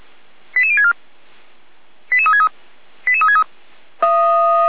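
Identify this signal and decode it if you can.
Five-tone selective call (ZVEI), heard as quick runs of five tones, tones of about 80 ms. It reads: 88064 88363 88363
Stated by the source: Received on an SDR; ZVEI five-tone call 88063.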